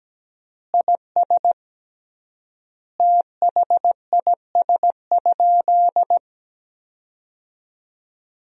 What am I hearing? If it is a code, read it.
Morse code, heard as a keyed tone, about 17 wpm, 700 Hz. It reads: IS THIS?
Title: 3x Word-Spacing - Sentences from Top 100 Words - 17wpm